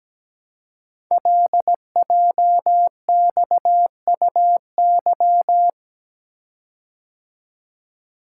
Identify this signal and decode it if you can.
Morse code, heard as a keyed tone, about 17 wpm, 700 Hz. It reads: LJXUY
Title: Code Groups - 5 Letter:Numbers - 17wpm